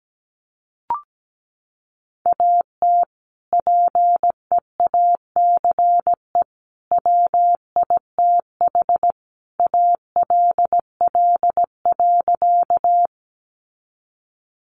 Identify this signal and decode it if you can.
Morse code, heard as a keyed tone, about 17 wpm, 700 Hz. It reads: AT PEACE WITH ALL.
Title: Sentences from Top 1500 Words 17wpm